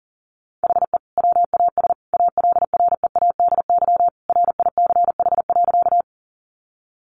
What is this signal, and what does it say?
Morse code, heard as a keyed tone, about 40 wpm, 700 Hz. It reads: HE WAS ALREADY RICH.
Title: Sentences from Top 1100 Words 40wpm